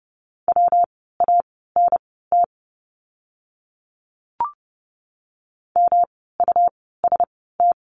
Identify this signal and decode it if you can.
Morse code, heard as a keyed tone, about 30 wpm, spaced as for 17 wpm, 700 Hz. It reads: WANT MUST
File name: course / 30:10wpm / 237 Lesson 38f - Letter q - Words 30wpm